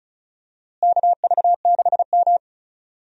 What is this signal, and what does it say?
Morse code, heard as a keyed tone, about 36 wpm, 700 Hz. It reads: KV6M
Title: USA Call Signs - 4 Characters - Repeated 3x - 35wpm